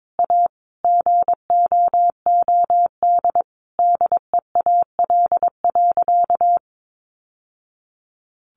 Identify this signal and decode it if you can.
Morse code, heard as a keyed tone, about 22 wpm, 700 Hz. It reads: A GOOD DEAL.
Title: Sentences from Top 1200 Words 22wpm